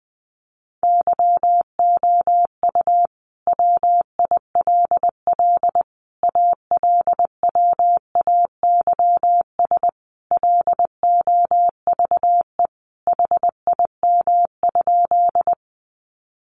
Morse code, heard as a keyed tone, about 20 wpm, 700 Hz. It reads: YOU WILL ALWAYS LOVE HIM?
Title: Sentences from Top 400 Words 20wpm